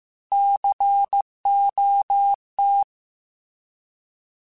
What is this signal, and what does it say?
Morse code, heard as a keyed tone, about 15 wpm, 800 Hz. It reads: COT
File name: Alphabetical / Alphabetical mp3s / Cs @13